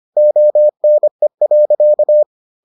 Morse code, heard as a keyed tone, about 25 wpm, 600 Hz. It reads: ONE.